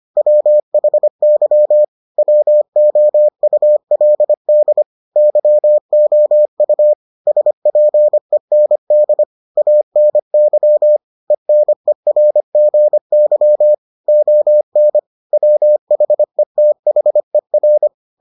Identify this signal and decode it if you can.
Morse code, heard as a keyed tone, about 25 wpm, 600 Hz. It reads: WHY WOULD YOU SPEND ANY ENERGY ON WHETHER